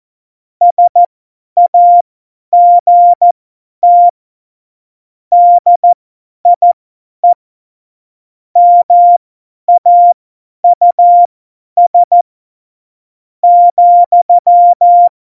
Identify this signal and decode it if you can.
Morse code, heard as a keyed tone, about 14 wpm, spaced as for 10 wpm, 700 Hz. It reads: SAGT DIE MAUS ,
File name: Morsekurs / Witze / Witz 11. Dezember, 50 F70